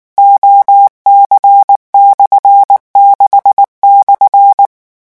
Morse code, heard as a keyed tone, about 19 wpm, 800 Hz. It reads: OC/6/
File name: morse mp3 / revision12-vitesse 18 mots